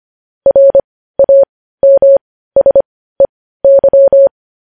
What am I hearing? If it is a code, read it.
Morse code, heard as a keyed tone, about 25 wpm, spaced as for 15 wpm, 550 Hz. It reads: RAMSEY
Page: news headlines in morse code at 15 wpm